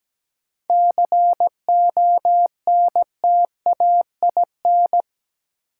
Morse code, heard as a keyed tone, about 17 wpm, 700 Hz. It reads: CONTAIN
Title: Sentences from Top 500 Words - New Word Drill - 17wpm